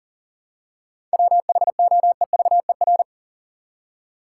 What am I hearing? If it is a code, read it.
Morse code, heard as a keyed tone, about 40 wpm, 700 Hz. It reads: WHOEVER
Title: Seven Letter Words - 40wpm